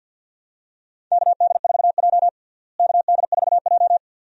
Morse code, heard as a keyed tone, about 47 wpm, 700 Hz. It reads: KD4J KD4J
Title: USA Call Signs - 4 Characters - Repeated 2x - 50wpm